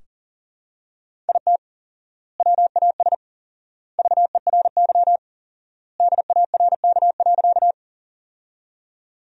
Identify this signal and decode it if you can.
Morse code, heard as a keyed tone, about 40 wpm, 700 Hz. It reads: IT WAS VERY DARK.
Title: 4x Word-Spacing - Sentences from Top 500 Words - 40wpm